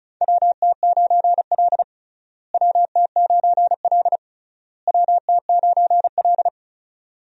Morse code, heard as a keyed tone, about 36 wpm, 700 Hz. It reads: WT9L WT9L WT9L
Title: USA Call Signs - 4 Characters - Repeated 3x - 35wpm